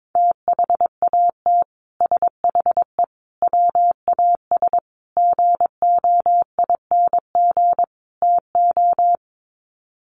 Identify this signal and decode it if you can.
Morse code, heard as a keyed tone, about 22 wpm, 700 Hz. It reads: THAT SHE WAS GOING TO
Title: Sets of 5 Words Encore2 22wpm